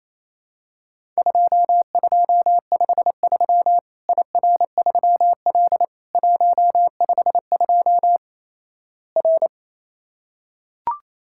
Morse code, heard as a keyed tone, about 28 wpm, 700 Hz. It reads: 2253 IR3L 152 R
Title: Worked All Europe DX Contest - QTC Exchange - 28wpm